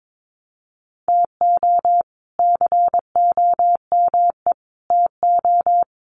Morse code, heard as a keyed tone, about 22 wpm, 700 Hz. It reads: TO COME TO
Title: Sentences from Top 900 Words 22wpm